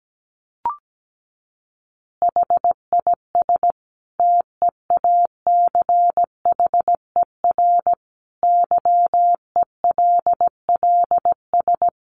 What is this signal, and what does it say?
Morse code, heard as a keyed tone, about 17 wpm, 700 Hz. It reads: HIS TEACHER YELLS
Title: Additional Easy Phrases 17wpm